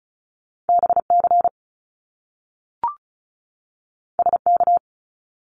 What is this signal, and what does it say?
Morse code, heard as a keyed tone, about 36 wpm, 700 Hz. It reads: BC SK
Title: Canadian Provinces and Territories 35wpm